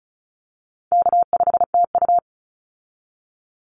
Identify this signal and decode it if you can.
Morse code, heard as a keyed tone, about 34 wpm, 700 Hz. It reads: K5TU
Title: Call Signs Repeated 3x - 35wpm